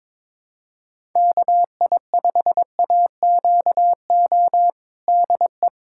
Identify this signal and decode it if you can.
Morse code, heard as a keyed tone, about 22 wpm, 700 Hz. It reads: KI5AQO DE
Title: USA Callsign DE Callsign 22wpm